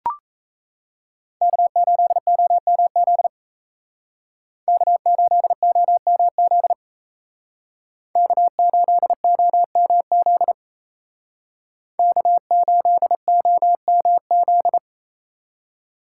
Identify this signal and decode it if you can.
Morse code, heard as a keyed tone, about 35 wpm, 700 Hz. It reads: K8OMZ K8OMZ K8OMZ K8OMZ